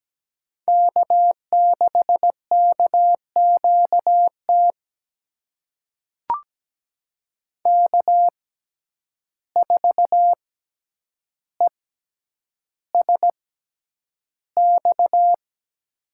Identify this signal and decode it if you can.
Morse code, heard as a keyed tone, about 17 wpm, spaced as for 9 wpm, 700 Hz. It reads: K6KQT K4ESX
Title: USA Call Signs - 5 Characters - 6x Character Spacing - 17wpm